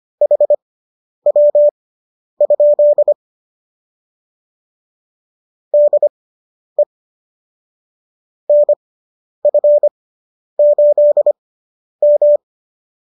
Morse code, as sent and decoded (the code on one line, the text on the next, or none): .... .-- ..--.. / -.. / -. ..-. ---.. --
HW? DE NF8M